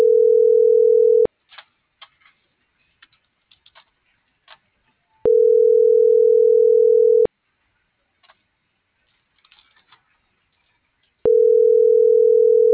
Ringback tone (North America), 440 + 480 Hz.